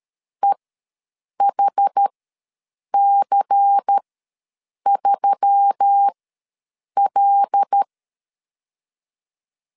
Morse code, heard as a keyed tone, about 13 wpm, spaced as for 7 wpm, 800 Hz. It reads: EHC3L
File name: LEZIONI / 6 PKX0 / 6 PK ws 12wpm 3116